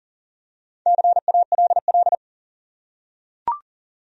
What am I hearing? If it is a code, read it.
Morse code, heard as a keyed tone, about 40 wpm, 700 Hz. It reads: CALL
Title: Top 100 Words 40wpm